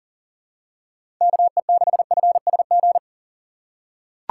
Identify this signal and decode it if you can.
Morse code, heard as a keyed tone, about 37 wpm, 700 Hz. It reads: KE6FSG
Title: USA Call Signs - 6 Characters - 3x Character Spacing - 40wpm